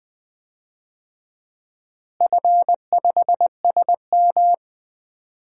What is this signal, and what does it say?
Morse code, heard as a keyed tone, about 20 wpm, 700 Hz. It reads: F5SM
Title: DXCC-100 Call Signs - 4 Characters - Repeated 2x - 20wpm